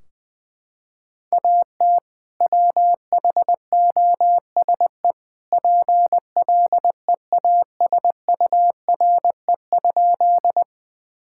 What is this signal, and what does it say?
Morse code, heard as a keyed tone, about 21 wpm, 700 Hz. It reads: AT WHOSE PLEASURE?
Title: Sentences from Top 1600 Words 20wpm